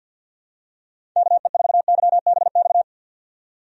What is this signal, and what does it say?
Morse code, heard as a keyed tone, about 52 wpm, 700 Hz. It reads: KE4YBX
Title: USA Call Signs - 6 Characters - 50wpm